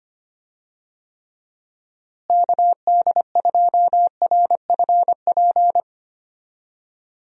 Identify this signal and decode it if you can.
Morse code, heard as a keyed tone, about 25 wpm, 700 Hz. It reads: KD2RFP
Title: USA Call Signs - 6 Characters - Repeated 3x - 25wpm